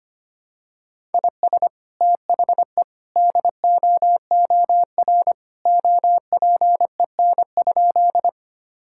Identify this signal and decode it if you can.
Morse code, heard as a keyed tone, about 25 wpm, 700 Hz. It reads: IS THE DOOR OPEN?
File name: practice / 25wpm / Sentences with 2-Word Phrases - 25wpm